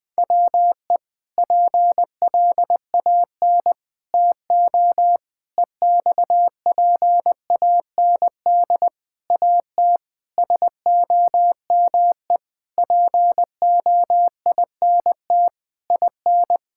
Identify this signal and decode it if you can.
Morse code, heard as a keyed tone, about 21 wpm, 700 Hz. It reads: WE PLAN TO EXPAND AT SOME POINT IN
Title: Sentences with 6-Word Phrases - 20wpm